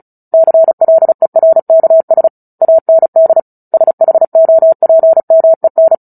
Morse code, heard as a keyed tone, about 34 wpm, 650 Hz. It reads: CLERKS AND SHOPMEN
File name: Book Two / 09-Chapter 9 - 35 WPM